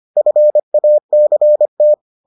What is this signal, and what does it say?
Morse code, heard as a keyed tone, about 25 wpm, 600 Hz. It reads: FACT